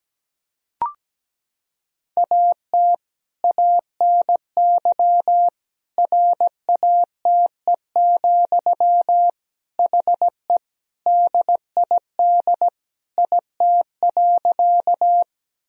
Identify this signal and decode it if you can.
Morse code, heard as a keyed tone, about 17 wpm, 700 Hz. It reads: AT ANY RATE, HE DID IT.
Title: Sentences from Top 1000 Words - Review Part 4 - 17wpm